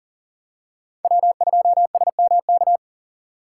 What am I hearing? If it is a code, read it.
Morse code, heard as a keyed tone, about 40 wpm, 700 Hz. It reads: W2SMK
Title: USA Call Signs - 5 Characters - 2x Character Spacing - 40wpm